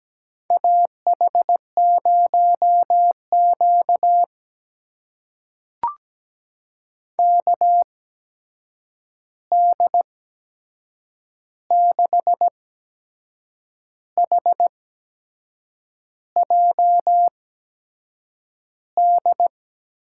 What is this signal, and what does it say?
Morse code, heard as a keyed tone, about 17 wpm, spaced as for 6 wpm, 700 Hz. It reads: AH0Q KD6HJD
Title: USA Call Signs - 8x Character Spacing - 17wpm